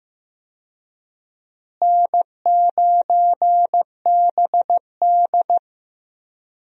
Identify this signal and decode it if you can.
Morse code, heard as a keyed tone, about 15 wpm, 700 Hz. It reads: N9BD
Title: USA Call Signs - 4 Characters - Repeated 2x - 15wpm